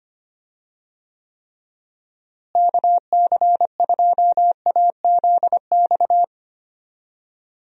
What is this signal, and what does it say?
Morse code, heard as a keyed tone, about 25 wpm, 700 Hz. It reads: KC2AZX
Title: USA Call Signs - 6 Characters - Repeated 3x - 25wpm